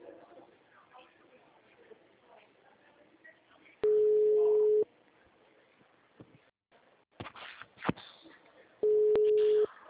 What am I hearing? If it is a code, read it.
Ringback tone (Europe, 425 Hz), with speech.